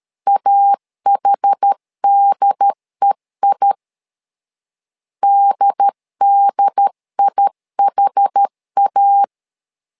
Morse code, heard as a keyed tone, about 13 wpm, 800 Hz. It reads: AHDEI DDIHA
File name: LEZIONI / 2 UHDN / 2 UHD 12wpm 1540